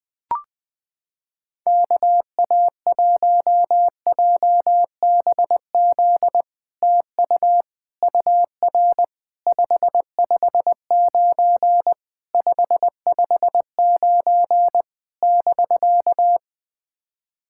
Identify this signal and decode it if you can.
Morse code, heard as a keyed tone, about 21 wpm, 700 Hz. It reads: KA1JBZ TU UR 559 559 <BK>
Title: SOTA QSO Practice 20wpm